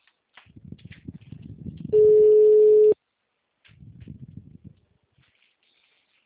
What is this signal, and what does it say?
Ringback tone (Europe, 425 Hz).